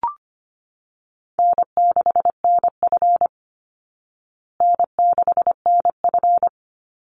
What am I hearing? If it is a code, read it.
Morse code, heard as a keyed tone, about 25 wpm, 700 Hz. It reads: N6NF N6NF
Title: Call Signs Repeated 2x - 25wpm